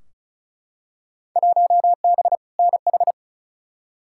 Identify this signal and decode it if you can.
Morse code, heard as a keyed tone, about 36 wpm, 700 Hz. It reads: 1B NH
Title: Field Day Exchanges Encore - 35wpm